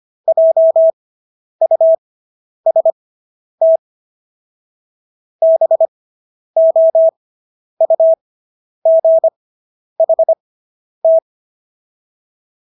Morse code, heard as a keyed tone, about 25 wpm, spaced as for 10 wpm, 650 Hz. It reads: JUST BOUGHT